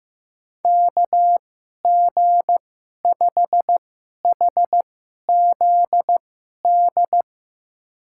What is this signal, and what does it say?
Morse code, heard as a keyed tone, about 15 wpm, spaced as for 11 wpm, 700 Hz. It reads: KG5HZD
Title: USA Call Signs - 6 Characters - 2x Character Spacing - 15wpm